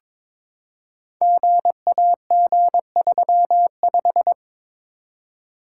Morse code, heard as a keyed tone, about 22 wpm, 690 Hz. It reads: GAG35
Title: Code Groups - 5 Letter:Numbers - 22wpm